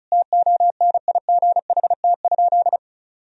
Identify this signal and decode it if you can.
Morse code, heard as a keyed tone, about 36 wpm, 700 Hz. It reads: TONIGHT?
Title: Sentences from Top 1000 Words - Review Part 2 - 35wpm